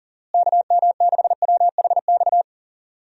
Morse code, heard as a keyed tone, about 40 wpm, 700 Hz. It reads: KM6WHX
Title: USA Call Signs - 6 Characters - Repeated 3x - 40wpm